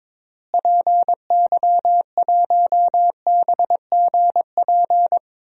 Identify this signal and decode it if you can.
Morse code, heard as a keyed tone, about 22 wpm, 700 Hz. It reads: PY1BGP
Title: DXCC-100 Call Signs - 6 Characters - Repeated 3x - 22wpm